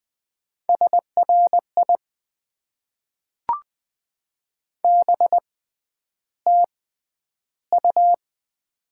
Morse code, heard as a keyed tone, about 20 wpm, spaced as for 10 wpm, 700 Hz. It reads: SRI BTU
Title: Top 90 QSO Elements - 6x Character Spacing - 20wpm